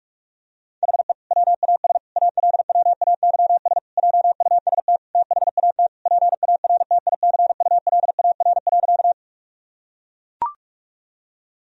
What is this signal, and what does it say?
Morse code, heard as a keyed tone, about 41 wpm, 700 Hz. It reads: HE WAS ALWAYS JUST THAT PARTICULAR.
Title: Sentences from Top 1000 Words - Review Part 4 - 45wpm